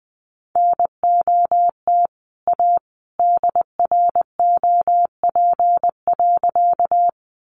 Morse code, heard as a keyed tone, about 20 wpm, 700 Hz. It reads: NOT A DROP.
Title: Sentences from Top 1400 Words 20wpm